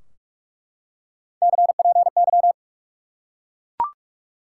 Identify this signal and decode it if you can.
Morse code, heard as a keyed tone, about 43 wpm, 700 Hz. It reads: CPY
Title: Top 90 QSO Elements 45wpm